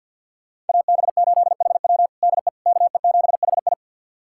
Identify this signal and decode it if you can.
Morse code, heard as a keyed tone, about 50 wpm, 700 Hz. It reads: AB8HW DE KE7HI
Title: USA Callsign DE Callsign - 5 Characters - 50wpm